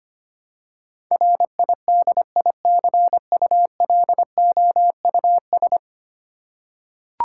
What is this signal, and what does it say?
Morse code, heard as a keyed tone, about 25 wpm, 700 Hz. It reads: RIDICULOUS
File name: 10 Letter Words - First 3 Letters Given - 25wpm